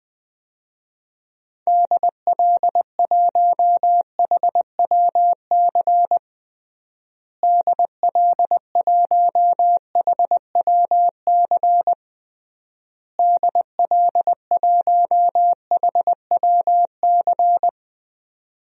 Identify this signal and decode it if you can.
Morse code, heard as a keyed tone, about 20 wpm, 700 Hz. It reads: DL1HWC DL1HWC DL1HWC
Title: DXCC-100 Call Signs - 6 Characters - Repeated 3x - 20wpm